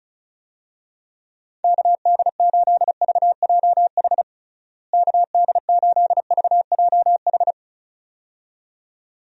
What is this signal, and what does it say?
Morse code, heard as a keyed tone, about 36 wpm, 700 Hz. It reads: KD8VJH KD8VJH
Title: USA Call Signs - 6 Characters - Repeated 2x - 35wpm